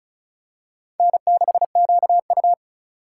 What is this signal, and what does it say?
Morse code, heard as a keyed tone, about 36 wpm, 700 Hz. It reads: N6QU
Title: USA Call Signs - 4 Characters - 4x Character Spacing - 35wpm